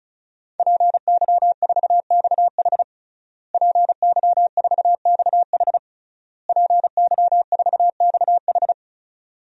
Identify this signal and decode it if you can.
Morse code, heard as a keyed tone, about 34 wpm, 700 Hz. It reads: PY4XH PY4XH PY4XH
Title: DXCC-100 Call Signs - 5 Characters - Repeated 3x - 35wpm